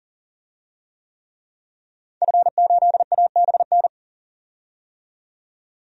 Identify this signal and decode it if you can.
Morse code, heard as a keyed tone, about 40 wpm, 700 Hz. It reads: F8ABN
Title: DXCC-100 Call Signs - 5 Characters - 40wpm